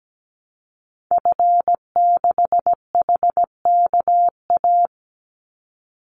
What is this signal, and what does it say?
Morse code, heard as a keyed tone, about 17 wpm, 700 Hz. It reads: F6HKA
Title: Call Signs Repeated 2x - 17wpm